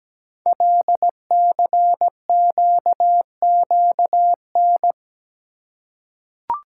Morse code, heard as a keyed tone, about 17 wpm, 700 Hz. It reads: LCQQN
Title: Code Groups - 5 Letters - 17wpm